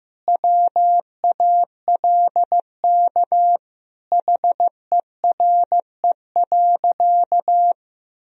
Morse code, heard as a keyed tone, about 15 wpm, 700 Hz. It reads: WALK HERE.
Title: Sentences from Top 1000 Words - Review Part 1 - 15wpm